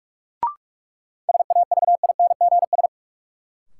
Morse code, heard as a keyed tone, about 46 wpm, 700 Hz. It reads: SAVINGS
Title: Sentences from Top 1200 Words - New Word Drill Alt - 45wpm